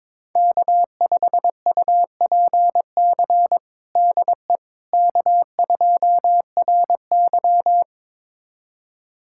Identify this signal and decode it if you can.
Morse code, heard as a keyed tone, about 21 wpm, 700 Hz. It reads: K5UPC DE K2RY
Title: USA Callsign DE Callsign 22wpm